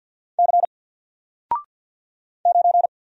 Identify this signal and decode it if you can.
Morse code, heard as a keyed tone, about 47 wpm, 700 Hz. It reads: C 9
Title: Single Letter-Number Alt 50wpm